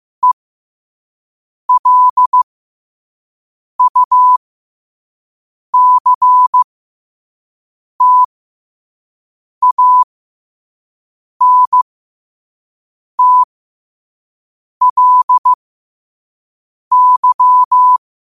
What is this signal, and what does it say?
Morse code, heard as a keyed tone, about 15 wpm, spaced as for 5 wpm, 1000 Hz. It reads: ELUCTANTLY